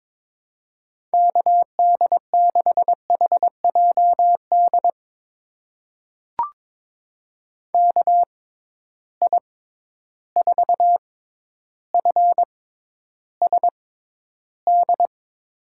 Morse code, heard as a keyed tone, about 22 wpm, spaced as for 11 wpm, 700 Hz. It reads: KD6HJD KI4FSD
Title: USA Call Signs - 6 Characters - 6x Character Spacing - 22wpm